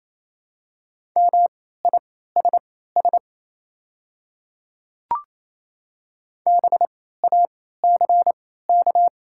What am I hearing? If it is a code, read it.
Morse code, heard as a keyed tone, about 28 wpm, spaced as for 16 wpm, 690 Hz. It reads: MISS BACK
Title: 3x Character Spacing - Top 500 Words - 28wpm